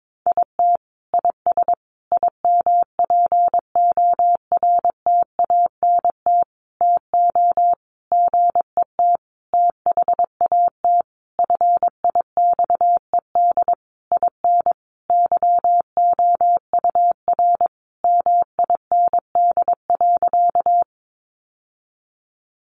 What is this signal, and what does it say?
Morse code, heard as a keyed tone, about 22 wpm, 700 Hz. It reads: IT IS IMPORTANT TO GET THAT FIXED IN YOUR MIND.
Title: Sentences from Top 1100 Words 22wpm